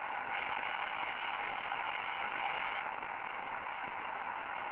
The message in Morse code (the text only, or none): none